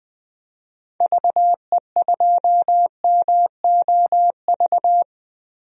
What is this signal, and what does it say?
Morse code, heard as a keyed tone, about 21 wpm, 700 Hz. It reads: VE2MOV
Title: DXCC-100 Call Signs - 6 Characters - Repeated 3x - 20wpm